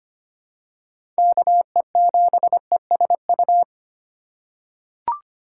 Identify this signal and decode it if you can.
Morse code, heard as a keyed tone, about 25 wpm, 700 Hz. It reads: KE7ESU
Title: USA Call Signs - 6 Characters - 6x Character Spacing - 25wpm